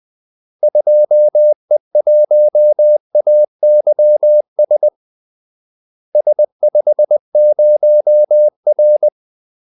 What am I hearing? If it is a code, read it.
Morse code, heard as a keyed tone, about 20 wpm, 600 Hz. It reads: 2E1AYS S50R
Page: Kaldesignaler 20 wpm